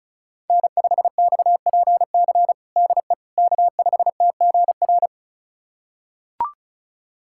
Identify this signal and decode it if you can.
Morse code, heard as a keyed tone, about 36 wpm, 700 Hz. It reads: N5XPC DE K5TGR
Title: USA Callsign DE Callsign - 5 Characters - 35wpm